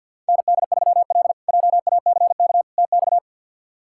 Morse code, heard as a keyed tone, about 47 wpm, 700 Hz. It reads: ND3L JACK TX
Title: K1USN SST QSO Practice - 50wpm